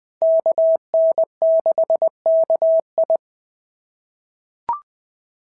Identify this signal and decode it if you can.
Morse code, heard as a keyed tone, about 20 wpm, 650 Hz. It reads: KN6KI
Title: POTA QSO Practice - 20wpm